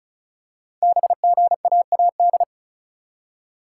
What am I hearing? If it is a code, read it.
Morse code, heard as a keyed tone, about 36 wpm, 690 Hz. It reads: BGAAD
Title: Code Groups - 5 Letters - 35wpm